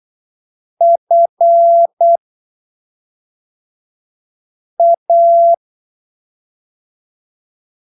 Morse code, heard as a keyed tone, about 8 wpm, spaced as for 3 wpm, 680 Hz. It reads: FA
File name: arquivos / texto / texto 28 08ppm